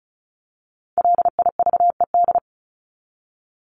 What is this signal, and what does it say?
Morse code, heard as a keyed tone, about 34 wpm, 700 Hz. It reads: LIVED
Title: ED Suffix - Top 100 Words with Base Words 35wpm